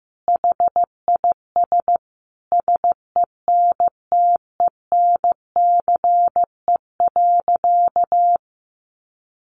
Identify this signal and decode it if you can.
Morse code, heard as a keyed tone, about 15 wpm, 700 Hz. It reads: HIS SENTENCE.
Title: Sentences from Top 1200 Words 15wpm